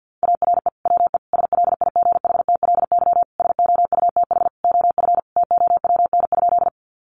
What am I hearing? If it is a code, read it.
Morse code, heard as a keyed tone, about 50 wpm, 700 Hz. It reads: ARE WE SLIGHTLY SOUTH OF TOWN?
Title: Sentences from Top 1300 Words 50wpm